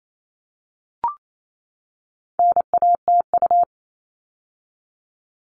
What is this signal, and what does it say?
Morse code, heard as a keyed tone, about 28 wpm, 700 Hz. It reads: NATU
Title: All Cut Numbers - 4-Digit 28wpm